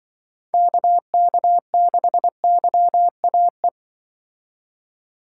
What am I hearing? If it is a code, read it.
Morse code, heard as a keyed tone, about 24 wpm, 700 Hz. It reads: KK6YAE
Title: Speed Racing - USA 2x3 Call Signs 20wpm